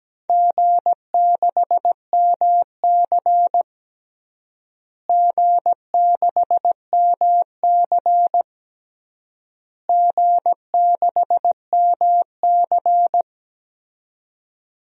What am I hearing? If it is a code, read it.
Morse code, heard as a keyed tone, about 17 wpm, 700 Hz. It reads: G6MC G6MC G6MC
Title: DXCC-100 Call Signs - 4 Characters - Repeated 3x - 17wpm